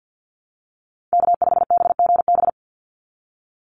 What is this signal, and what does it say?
Morse code, heard as a keyed tone, about 50 wpm, 700 Hz. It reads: K5BGB
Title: Call Signs Repeated 3x - 50wpm